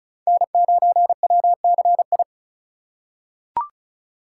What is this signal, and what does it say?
Morse code, heard as a keyed tone, about 34 wpm, 700 Hz. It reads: N9WCI
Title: USA Call Signs - 5 Characters - 4x Character Spacing - 35wpm